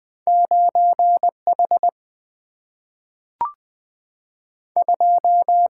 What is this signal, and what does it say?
Morse code, heard as a keyed tone, about 20 wpm, 700 Hz. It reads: 9 H 2
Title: Single-Letter Single-Number 20wpm